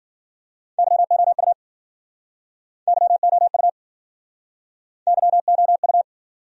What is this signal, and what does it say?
Morse code, heard as a keyed tone, about 53 wpm, 700 Hz. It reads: YOU YOU YOU